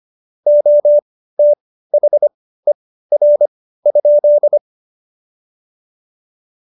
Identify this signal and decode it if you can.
Morse code, heard as a keyed tone, about 25 wpm, spaced as for 15 wpm, 590 Hz. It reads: OTHER?
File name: Jokes-25-char-15-eff